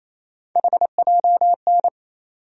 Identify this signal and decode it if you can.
Morse code, heard as a keyed tone, about 28 wpm, 700 Hz. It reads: HJN
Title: USA Call Sign Postfixes - 3 Letters 28wpm